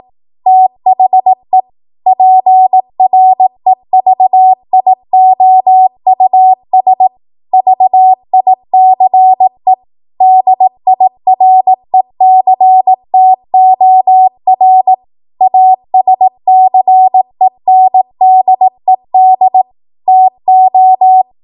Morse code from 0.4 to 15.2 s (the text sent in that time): THE PREVIOUS VICE DIRECTOR